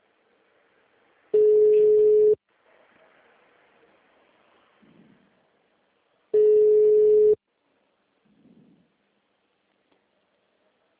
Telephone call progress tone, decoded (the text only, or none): ringback tone